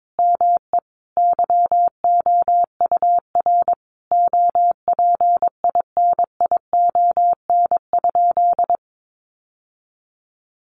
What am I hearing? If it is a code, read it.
Morse code, heard as a keyed tone, about 22 wpm, 700 Hz. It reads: ME YOUR OPINION?
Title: Sentences from Top 1200 Words 22wpm